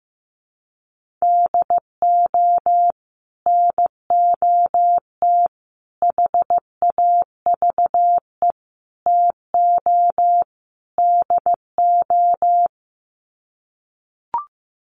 Morse code, heard as a keyed tone, about 15 wpm, 700 Hz. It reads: DO NOT HAVE TO DO
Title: Sets of 5 Words 15wpm